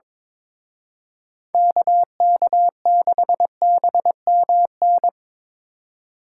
Morse code, heard as a keyed tone, about 24 wpm, 700 Hz. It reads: KK6BMN
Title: Speed Racing - USA 2x3 Call Signs 22wpm